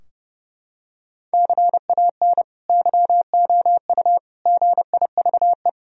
Morse code, heard as a keyed tone, about 32 wpm, 700 Hz. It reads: CAN YOU GIVE
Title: Sentences from Top 1000 Words - Review Part 3 - 30wpm